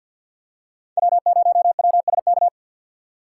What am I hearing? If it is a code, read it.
Morse code, heard as a keyed tone, about 48 wpm, 700 Hz. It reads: W0WSK
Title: USA Call Signs - 4x Character Spacing - 50wpm